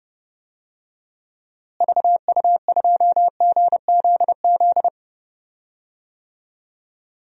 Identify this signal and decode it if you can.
Morse code, heard as a keyed tone, about 30 wpm, 700 Hz. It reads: VU2GZZ
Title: DXCC-100 Call Signs - 6 Characters - Repeated 3x - 30wpm